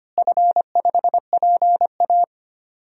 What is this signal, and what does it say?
Morse code, heard as a keyed tone, about 25 wpm, 700 Hz. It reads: F5PA
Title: DXCC-100 Call Signs - 4 Characters - Repeated 2x - 25wpm